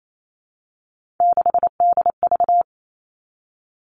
Morse code, heard as a keyed tone, about 28 wpm, 700 Hz. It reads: 6DV
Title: All Cut Numbers - 3-Digit 28wpm